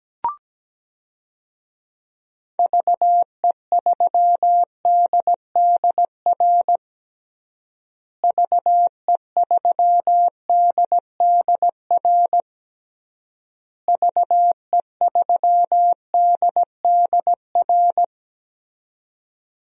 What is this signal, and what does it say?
Morse code, heard as a keyed tone, about 17 wpm, 700 Hz. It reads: VE3DDR VE3DDR VE3DDR
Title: DXCC-100 Call Signs - 6 Characters - Repeated 3x - 17wpm